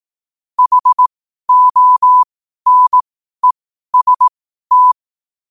Morse code, heard as a keyed tone, about 18 wpm, spaced as for 12 wpm, 1000 Hz. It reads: HONEST
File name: qotdfiles / 12wpm